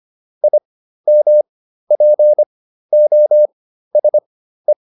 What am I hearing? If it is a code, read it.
Morse code, heard as a keyed tone, about 25 wpm, spaced as for 13 wpm, 600 Hz. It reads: IMPOSE